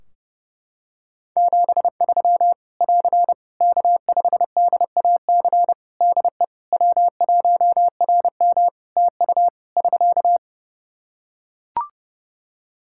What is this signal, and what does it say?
Morse code, heard as a keyed tone, about 30 wpm, 700 Hz. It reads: 73 <AR> K5DAC DE W1RM TU <SK>
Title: Practice QSO - Exchange 4 - Ending - 30wpm